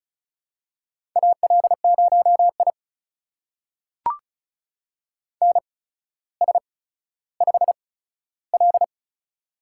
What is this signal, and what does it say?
Morse code, heard as a keyed tone, about 36 wpm, spaced as for 15 wpm, 700 Hz. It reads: AL0I NS5L
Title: USA Call Signs - 4 Characters - 8x Character Spacing - 35wpm